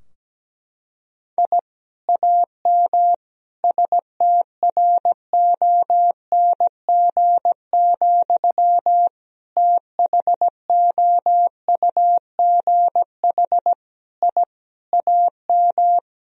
Morse code, heard as a keyed tone, about 17 wpm, 700 Hz. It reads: I AM STRONG, THOUGH I AM
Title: Sentences from Top 1000 Words - Review Part 3 - 17wpm